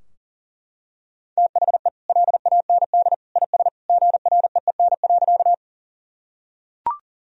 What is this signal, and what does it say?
Morse code, heard as a keyed tone, about 40 wpm, 700 Hz. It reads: THE LAND IS GREEN.
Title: Sentences from Top 1000 Words - Review Part 2 - 40wpm